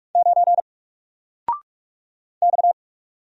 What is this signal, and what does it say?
Morse code, heard as a keyed tone, about 44 wpm, 700 Hz. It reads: 9 X
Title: Single Letter-Number 45wpm